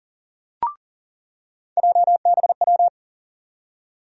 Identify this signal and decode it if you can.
Morse code, heard as a keyed tone, about 40 wpm, 700 Hz. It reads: JBW
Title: USA Call Sign Postfixes - 3 Letters 40wpm